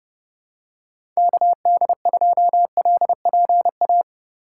Morse code, heard as a keyed tone, about 32 wpm, 700 Hz. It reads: KD2LPA
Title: USA Call Signs - 6 Characters - 3x Character Spacing - 30wpm